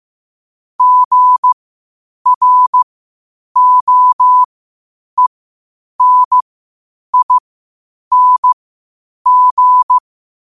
Morse code, heard as a keyed tone, about 15 wpm, spaced as for 8 wpm, 1000 Hz. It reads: GROENING
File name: qotdfiles / 8wpm